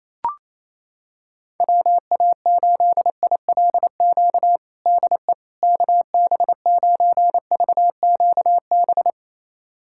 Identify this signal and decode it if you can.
Morse code, heard as a keyed tone, about 27 wpm, 700 Hz. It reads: WA8ILQ DE KB9VQB
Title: USA Callsign DE Callsign 28wpm